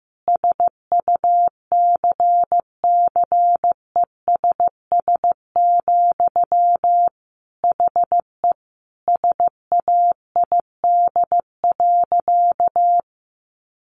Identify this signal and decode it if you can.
Morse code, heard as a keyed tone, about 15 wpm, 700 Hz. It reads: SUCCESS, HE SAID.
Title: Sentences from Top 900 Words 15wpm